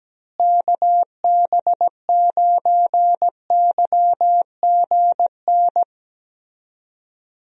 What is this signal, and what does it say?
Morse code, heard as a keyed tone, about 17 wpm, 700 Hz. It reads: KB9YGN
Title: USA Call Signs - 6 Characters - Repeated 3x - 17wpm